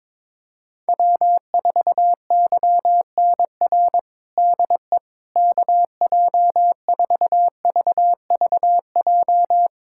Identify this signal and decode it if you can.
Morse code, heard as a keyed tone, about 22 wpm, 700 Hz. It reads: W4YNR DE KJ4VVJ
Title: USA Callsign DE Callsign 22wpm